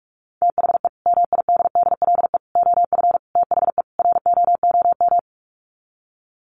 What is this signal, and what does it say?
Morse code, heard as a keyed tone, about 44 wpm, 700 Hz. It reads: THE MIDDLE OF THE ROOM
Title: Sets of 5 Words Encore2 45wpm